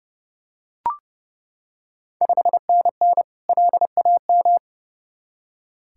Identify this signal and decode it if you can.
Morse code, heard as a keyed tone, about 30 wpm, 700 Hz. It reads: 5NN LAM
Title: Ontario QSO Party Exchange 30wpm